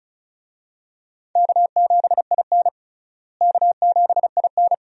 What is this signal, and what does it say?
Morse code, heard as a keyed tone, about 36 wpm, 700 Hz. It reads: K7IN K7IN
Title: USA Call Signs - 4 Characters - Repeated 2x - 35wpm